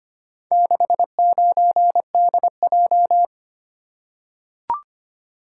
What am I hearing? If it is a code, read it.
Morse code, heard as a keyed tone, about 25 wpm, 700 Hz. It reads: 69DJ
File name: practice / 25wpm / Four Letter-Number Combinations - 25wpm